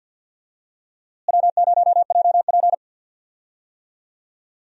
Morse code, heard as a keyed tone, about 46 wpm, 700 Hz. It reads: W0JP
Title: USA Call Signs - 4 Characters - Repeated 2x - 50wpm